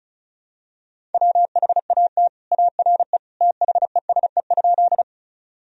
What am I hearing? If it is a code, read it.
Morse code, heard as a keyed tone, about 34 wpm, 700 Hz. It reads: WHAT ARE THESE?
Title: Sentences from Top 1000 Words - Review Part 1 - 35wpm